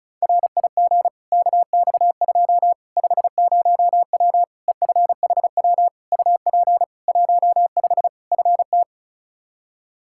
Morse code, heard as a keyed tone, about 36 wpm, 700 Hz. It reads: RIG KX2 50W EFHW UP 15 FT
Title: K1USN SST QSO Practice 35wpm